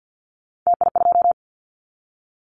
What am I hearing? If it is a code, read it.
Morse code, heard as a keyed tone, about 50 wpm, 700 Hz. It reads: TI2
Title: Callsign Prefixes for Top 100 DXCC - 50wpm